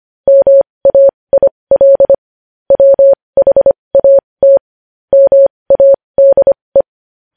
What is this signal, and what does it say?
Morse code, heard as a keyed tone, about 25 wpm, spaced as for 20 wpm, 550 Hz. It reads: MAIL WHAT MADE
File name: podcast / News headlines-1x-25x20